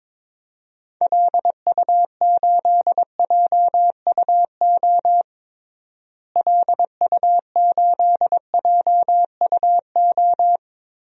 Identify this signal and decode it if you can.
Morse code, heard as a keyed tone, about 21 wpm, 700 Hz. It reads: LU8JUO LU8JUO
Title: DXCC-100 Call Signs - 6 Characters - Repeated 3x - 22wpm